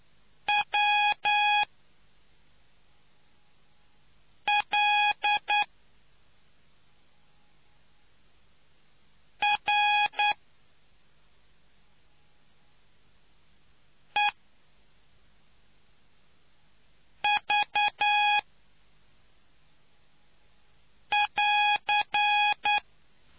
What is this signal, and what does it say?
Morse code, heard as a keyed tone, about 9 wpm, spaced as for 2 wpm, 830 Hz. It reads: WLREV <AR>